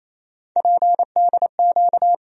Morse code, heard as a keyed tone, about 28 wpm, 700 Hz. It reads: PDQ
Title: USA Call Sign Postfixes - 3 Letters 28wpm